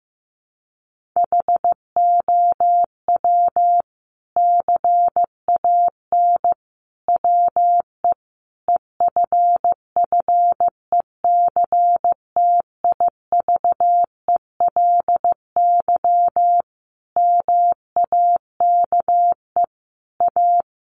Morse code, heard as a keyed tone, about 15 wpm, 700 Hz. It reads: HOW CAN WE EFFECTIVELY MAKE A
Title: Sentences from Top 1400 Words 15wpm